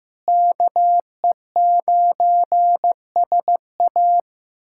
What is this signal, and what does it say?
Morse code, heard as a keyed tone, about 15 wpm, 700 Hz. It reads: KE9SA
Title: K1USN SST QSO Practice - 15wpm